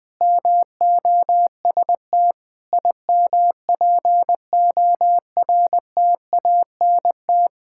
Morse code, heard as a keyed tone, about 20 wpm, 700 Hz. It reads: MOST IMPORTANT
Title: Sentences with 5-Word Phrases - 20wpm